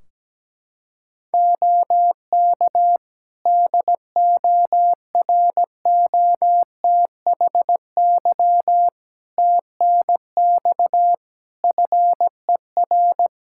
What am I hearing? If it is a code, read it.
Morse code, heard as a keyed tone, about 17 wpm, 700 Hz. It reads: OK DOROTHY TNX FER
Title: Practice QSO - Exchange 4 - Ending - 17wpm